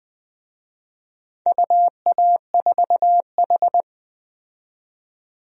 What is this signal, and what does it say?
Morse code, heard as a keyed tone, about 21 wpm, 700 Hz. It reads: UA4H
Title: DXCC-100 Call Signs - 4 Characters - 20wpm